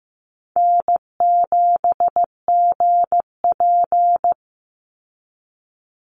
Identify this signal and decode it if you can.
Morse code, heard as a keyed tone, about 15 wpm, 700 Hz. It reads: N7GP
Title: USA Call Signs 15wpm